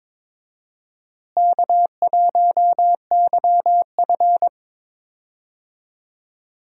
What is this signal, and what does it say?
Morse code, heard as a keyed tone, about 21 wpm, 700 Hz. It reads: K1YF
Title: USA Call Signs - 4 Characters - Repeated 3x - 22wpm